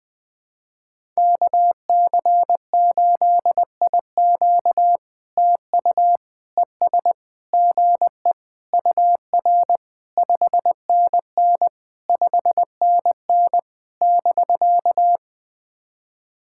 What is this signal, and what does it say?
Morse code, heard as a keyed tone, about 21 wpm, 700 Hz. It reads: KC8IQ TU ES GE UR 5NN 5NN <BK>
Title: POTA QSO Practice - 20wpm